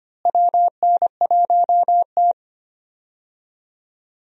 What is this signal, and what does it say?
Morse code, heard as a keyed tone, about 25 wpm, 700 Hz. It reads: WN1T
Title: USA Call Signs - 4 Characters - Repeated 3x - 25wpm